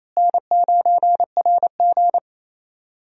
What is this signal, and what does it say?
Morse code, heard as a keyed tone, about 28 wpm, 700 Hz. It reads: N9RG
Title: USA Call Signs - 4 Characters - Repeated 3x - 28wpm